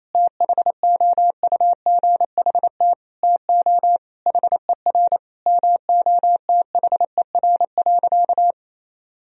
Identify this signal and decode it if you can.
Morse code, heard as a keyed tone, about 28 wpm, 700 Hz. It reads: THOUGHT TO HER MOTHER.
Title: Sentences from Top 1000 Words - Review Part 2 - 28wpm